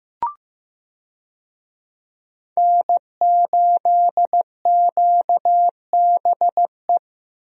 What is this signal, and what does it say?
Morse code, heard as a keyed tone, about 15 wpm, 700 Hz. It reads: N8QBE
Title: USA Call Signs - 5 Characters - Repeated 2x - 15wpm